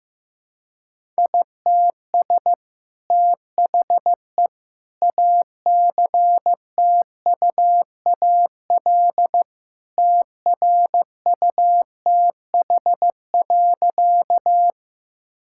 Morse code, heard as a keyed tone, about 16 wpm, 700 Hz. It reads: ITS THE ACTUAL TRUTH.
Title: Sentences from Top 1000 Words - Review Part 5 - 15wpm